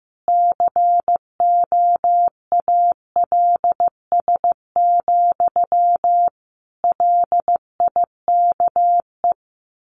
Morse code, heard as a keyed tone, about 15 wpm, 700 Hz. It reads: COALS, LIKE
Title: A Sound Of Thunder 15wpm